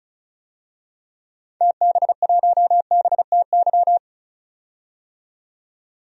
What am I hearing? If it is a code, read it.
Morse code, heard as a keyed tone, about 36 wpm, 700 Hz. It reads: TB1BTY
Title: DXCC-100 Call Signs - 6 Characters - 35wpm